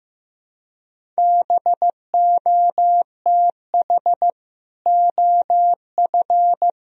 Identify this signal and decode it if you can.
Morse code, heard as a keyed tone, about 15 wpm, 700 Hz. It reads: BOTH OF
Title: Sentences from Top 1000 Words - Review Part 1 - 15wpm